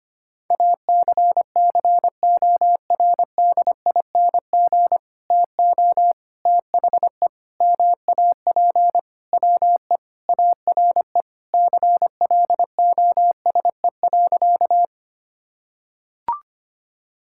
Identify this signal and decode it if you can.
Morse code, heard as a keyed tone, about 25 wpm, 700 Hz. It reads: ACCORDING TO THE MAP WE ARE CLOSE.
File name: practice / 25wpm / Sentences with 2-Word Phrases - 25wpm